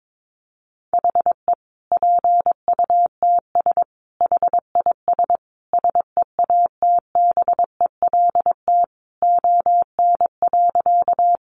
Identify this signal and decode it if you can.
Morse code, heard as a keyed tone, about 22 wpm, 700 Hz. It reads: HE PUTS HIS SEATBELT ON.
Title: Additional Easy Phrases 22wpm